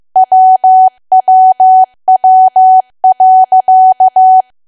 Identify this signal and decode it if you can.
Morse code, heard as a keyed tone, about 15 wpm, 750 Hz. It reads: WWW.
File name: cwa / W1AW-Level2 / 15WPM